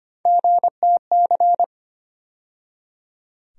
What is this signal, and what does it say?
Morse code, heard as a keyed tone, about 25 wpm, 700 Hz. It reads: GTC
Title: USA Call Sign Postfixes - 3 Letters 25wpm